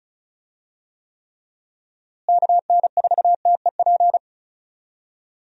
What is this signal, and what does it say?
Morse code, heard as a keyed tone, about 36 wpm, 700 Hz. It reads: KN4TEP
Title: USA Call Signs - 6 Characters - Repeated 3x - 35wpm